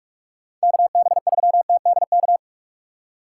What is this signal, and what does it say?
Morse code, heard as a keyed tone, about 55 wpm, 700 Hz. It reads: KB3TDK
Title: Speed Racing - USA 2x3 Call Signs 45wpm